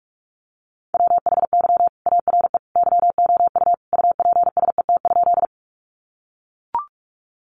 Morse code, heard as a keyed tone, about 44 wpm, 700 Hz. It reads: WHY ARE YOU UPSET?
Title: Sentences from Top 800 Words 45wpm